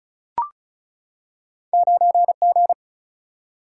Morse code, heard as a keyed tone, about 36 wpm, 700 Hz. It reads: 9 G
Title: Single-Letter Single-Number 35wpm